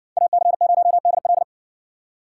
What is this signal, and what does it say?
Morse code, heard as a keyed tone, about 60 wpm, 700 Hz. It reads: AC0DL